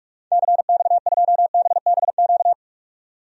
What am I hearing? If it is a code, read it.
Morse code, heard as a keyed tone, about 44 wpm, 700 Hz. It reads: CX2BBQ
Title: DXCC-100 Call Signs - 6 Characters - 45wpm